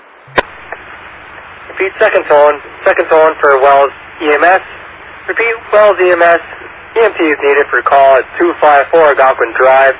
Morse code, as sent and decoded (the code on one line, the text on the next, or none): none